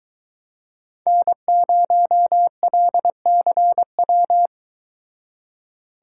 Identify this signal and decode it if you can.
Morse code, heard as a keyed tone, about 23 wpm, 700 Hz. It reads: N0LCW